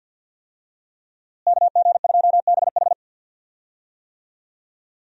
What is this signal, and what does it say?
Morse code, heard as a keyed tone, about 52 wpm, 700 Hz. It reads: KG2BH
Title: USA Call Signs - 5 Characters - Repeated 3x - 50wpm